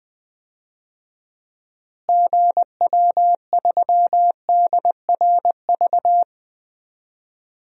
Morse code, heard as a keyed tone, about 21 wpm, 700 Hz. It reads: GW3DRV
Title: DXCC-100 Call Signs - 6 Characters - Repeated 3x - 20wpm